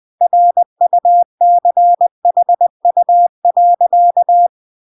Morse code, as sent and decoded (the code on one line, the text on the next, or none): .-. ..- -.-. .... ..- .-.-.-
RUCHU.